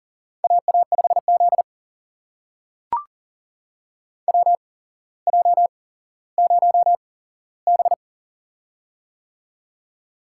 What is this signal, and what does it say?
Morse code, heard as a keyed tone, about 41 wpm, spaced as for 17 wpm, 700 Hz. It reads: AA5Z WJ0B
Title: USA Call Signs - 4 Characters - 8x Character Spacing - 40wpm